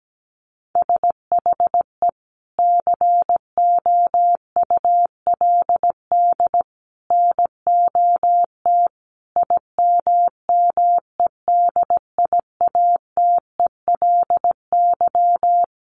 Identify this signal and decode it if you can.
Morse code, heard as a keyed tone, about 17 wpm, 700 Hz. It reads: SHE COULD NOT IMMEDIATELY
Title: Sentences from Top 1100 Words 17wpm